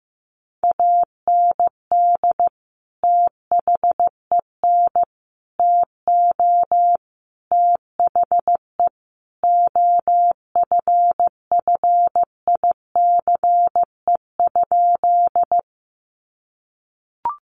Morse code, heard as a keyed tone, about 15 wpm, 700 Hz. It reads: AND THEN TO THE OFFICE?
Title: Sentences from Top 1000 Words 15wpm